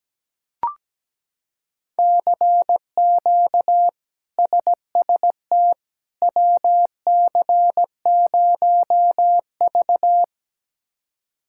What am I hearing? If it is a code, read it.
Morse code, heard as a keyed tone, about 17 wpm, 700 Hz. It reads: CQ SST WC0V
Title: K1USN SST QSO Practice - 17wpm